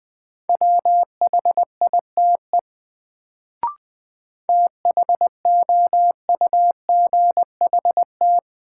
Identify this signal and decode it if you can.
Morse code, heard as a keyed tone, about 20 wpm, 700 Hz. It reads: WHITE THOUGHT
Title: Sets of 2 Words - New Word Drill - 20wpm